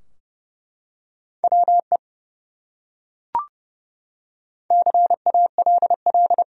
Morse code, heard as a keyed tone, about 30 wpm, 700 Hz. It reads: WE CALL
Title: Top 100 Words Repeated 30wpm